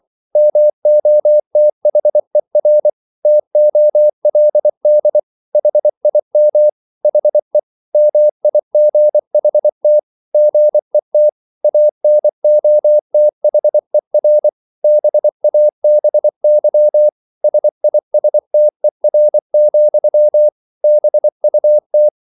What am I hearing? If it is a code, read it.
Morse code, heard as a keyed tone, about 24 wpm, 600 Hz. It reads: MOTHER TOLD HIM HE MIGHT GET ANOTHER BABY SISTER, BUT